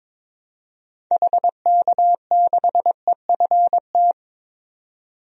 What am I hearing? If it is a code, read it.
Morse code, heard as a keyed tone, about 22 wpm, 700 Hz. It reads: HK6EFT
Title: DXCC-100 Call Signs - 6 Characters - 22wpm